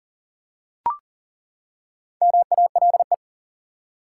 Morse code, heard as a keyed tone, about 40 wpm, 700 Hz. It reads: MALE
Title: Four Letter Words - 40wpm